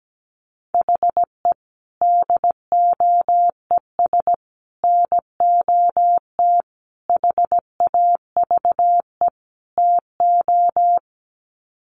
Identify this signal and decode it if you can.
Morse code, heard as a keyed tone, about 17 wpm, 700 Hz. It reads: HE DOES NOT HAVE TO
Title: Sets of 5 Words Encore 17wpm